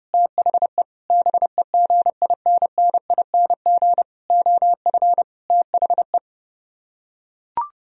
Morse code, heard as a keyed tone, about 30 wpm, 700 Hz. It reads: THE BEGINNING OF THE